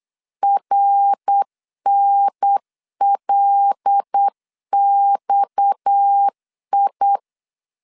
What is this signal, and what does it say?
Morse code, heard as a keyed tone, about 8 wpm, 800 Hz. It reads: RNLXI
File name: LEZIONI / 7 GF9Z / 7 GF 8wpm 3108